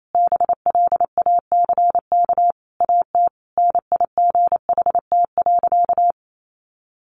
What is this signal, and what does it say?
Morse code, heard as a keyed tone, about 28 wpm, 700 Hz. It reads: BLACK AT NIGHT.
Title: Sentences from Top 1300 Words 28wpm